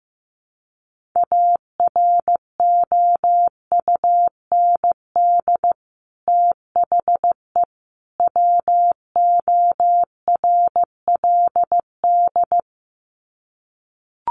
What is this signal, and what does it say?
Morse code, heard as a keyed tone, about 15 wpm, 700 Hz. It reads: AROUND THE WORLD
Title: Sets of 3 Words 15wpm